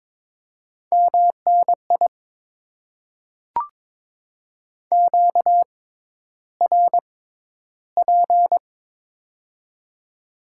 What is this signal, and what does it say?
Morse code, heard as a keyed tone, about 21 wpm, spaced as for 11 wpm, 700 Hz. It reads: MNI QRP
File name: Top 90 QSO Elements - 6x Character Spacing - 22wpm